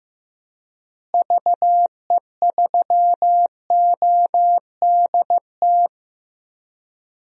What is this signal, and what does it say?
Morse code, heard as a keyed tone, about 15 wpm, 700 Hz. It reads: VE3ODT
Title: DXCC-100 Call Signs - 6 Characters - Repeated 3x - 15wpm